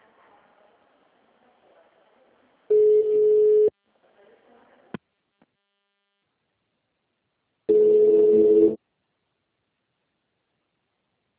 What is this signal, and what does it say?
Ringback tone (Europe, 425 Hz), with speech.
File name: out- -2004- - - .743